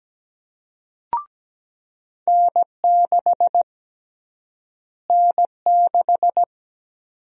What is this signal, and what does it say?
Morse code, heard as a keyed tone, about 17 wpm, 700 Hz. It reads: N6 N6
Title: USA Call Sign Prefixes - Repeated 2x - 17wpm